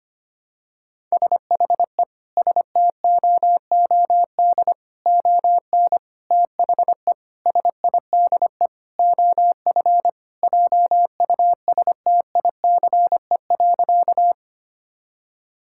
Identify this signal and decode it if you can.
Morse code, heard as a keyed tone, about 25 wpm, 700 Hz. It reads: SHE STOOD ON THE SIDE OF JUSTICE.
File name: practice / 25wpm / Sentences with 4-Word Phrases - 25wpm